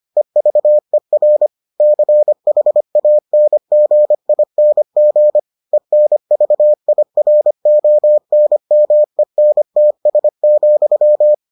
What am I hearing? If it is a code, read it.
Morse code, heard as a keyed tone, about 25 wpm, 600 Hz. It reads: EVER CHANGING ENVIRONMENTS,